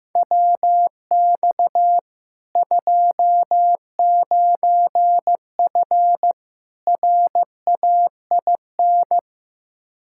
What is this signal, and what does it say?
Morse code, heard as a keyed tone, about 15 wpm, 700 Hz. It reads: WX 29F RAIN
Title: K1USN SST QSO Practice 15wpm